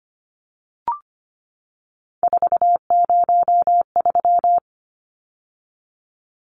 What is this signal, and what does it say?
Morse code, heard as a keyed tone, about 25 wpm, 700 Hz. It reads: 403
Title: Numbers - 3-Digit 25wpm